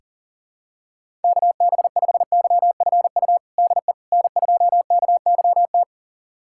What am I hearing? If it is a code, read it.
Morse code, heard as a keyed tone, about 40 wpm, 700 Hz. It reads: KB5YFU DE N2KYT
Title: USA Callsign DE Callsign 40wpm